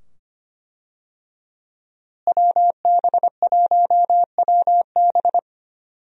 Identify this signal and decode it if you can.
Morse code, heard as a keyed tone, about 25 wpm, 700 Hz. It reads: WB1WB
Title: USA Call Signs - 5 Characters - Repeated 2x - 25wpm